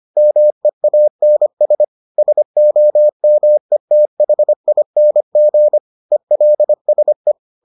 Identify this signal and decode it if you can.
Morse code, heard as a keyed tone, about 25 wpm, 600 Hz. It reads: MEANS SOMETHING ELSE